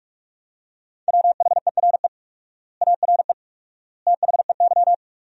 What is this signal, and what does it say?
Morse code, heard as a keyed tone, about 46 wpm, 700 Hz. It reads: WHERE ARE THEY
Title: 4x Word-Spacing - Sentences from Top 200 Words - 45wpm